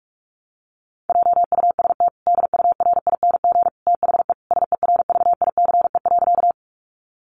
Morse code, heard as a keyed tone, about 44 wpm, 700 Hz. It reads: JUST DURING THE SERVICE.
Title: Sentences from Top 700 Words 45wpm